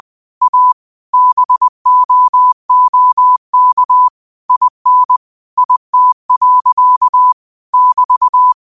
Morse code, heard as a keyed tone, about 20 wpm, 1000 Hz. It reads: A BOOK IN IT. <BT>
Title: 20 WPM morse code quotes for Thu, 29 Feb 2024